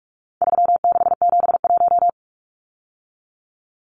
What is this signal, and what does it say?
Morse code, heard as a keyed tone, about 44 wpm, 700 Hz. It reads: 3671
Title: Numbers - 4-Digit 45wpm